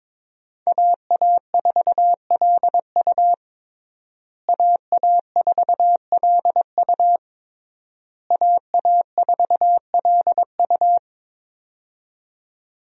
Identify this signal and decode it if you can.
Morse code, heard as a keyed tone, about 22 wpm, 700 Hz. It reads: AA4LU AA4LU AA4LU
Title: USA Call Signs - 5 Characters - Repeated 3x - 22wpm